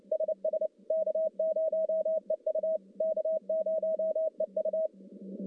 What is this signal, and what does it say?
Morse code, heard as a keyed tone, about 30 wpm, 640 Hz. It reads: SS K0EU K0EU
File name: contests / CW-SS-k5tr- -1100